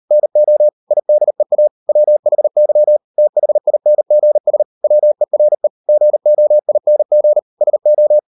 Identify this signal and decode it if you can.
Morse code, heard as a keyed tone, about 38 wpm, 600 Hz. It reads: NO IDEA WHY THINGS WERE GOING SO